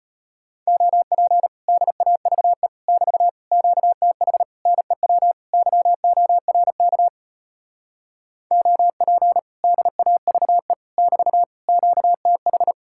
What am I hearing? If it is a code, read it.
Morse code, heard as a keyed tone, about 33 wpm, 700 Hz. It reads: OP DAVE <BT> QTH NEW YORK OP DAVE <BT> QTH